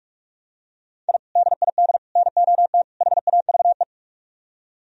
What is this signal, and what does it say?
Morse code, heard as a keyed tone, about 44 wpm, 700 Hz. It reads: I DID NOT HAVE